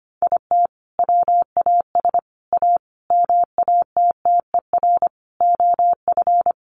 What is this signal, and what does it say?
Morse code, heard as a keyed tone, about 25 wpm, 700 Hz. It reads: IT WAS A MATTER OF
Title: Sentences from Top 1500 Words 25wpm